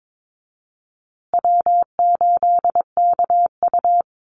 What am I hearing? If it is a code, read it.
Morse code, heard as a keyed tone, about 22 wpm, 700 Hz. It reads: W8KU
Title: USA Call Signs Repeated 2x 22wpm